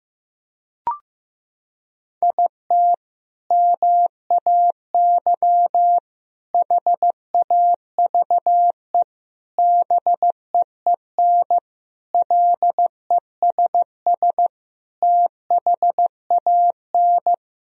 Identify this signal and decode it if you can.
Morse code, heard as a keyed tone, about 16 wpm, 700 Hz. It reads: IT MAY HAVE BEEN LESS THAN
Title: Sentences from Top 1000 Words - Review Part 2 - 15wpm